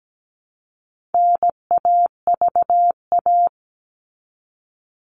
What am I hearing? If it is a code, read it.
Morse code, heard as a keyed tone, about 17 wpm, 700 Hz. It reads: NAVA
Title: All Cut Numbers - 4-Digit 17wpm